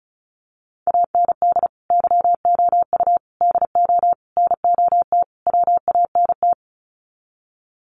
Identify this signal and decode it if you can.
Morse code, heard as a keyed tone, about 34 wpm, 700 Hz. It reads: AND YOU DO NOT WANT
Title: Sets of 5 Words Encore2 35wpm